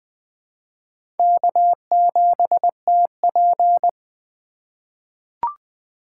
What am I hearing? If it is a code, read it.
Morse code, heard as a keyed tone, about 21 wpm, 700 Hz. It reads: K7TP
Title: USA Call Signs - 4 Characters - 2x Character Spacing - 20wpm